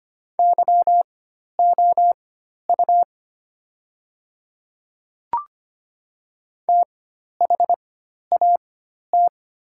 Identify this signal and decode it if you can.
Morse code, heard as a keyed tone, about 25 wpm, spaced as for 12 wpm, 700 Hz. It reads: YOU THAT
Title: 4x Character Spacing - Top 300 Words - 25wpm